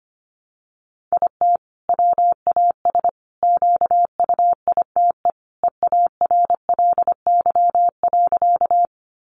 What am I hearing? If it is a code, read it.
Morse code, heard as a keyed tone, about 25 wpm, 700 Hz. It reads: IT WAS QUITE EARLY.
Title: Sentences from Top 700 Words 25wpm